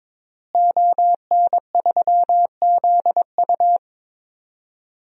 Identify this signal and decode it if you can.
Morse code, heard as a keyed tone, about 22 wpm, 700 Hz. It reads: ON3ZU
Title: DXCC-100 Call Signs - 5 Characters - Repeated 3x - 22wpm